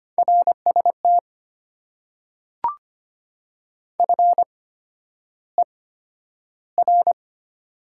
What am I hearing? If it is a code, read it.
Morse code, heard as a keyed tone, about 25 wpm, spaced as for 11 wpm, 700 Hz. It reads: RST FER
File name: Top 90 QSO Elements - 8x Character Spacing - 25wpm